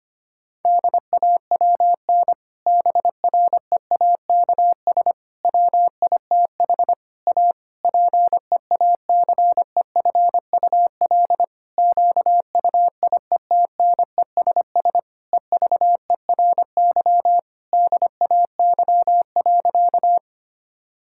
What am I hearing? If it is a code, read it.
Morse code, heard as a keyed tone, about 25 wpm, 700 Hz. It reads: DAWN BREAKS WITH A PEACEFUL QUIETNESS EVERY DAY.
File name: practice / 25wpm / Difficult Sentences with Hints - 25wpm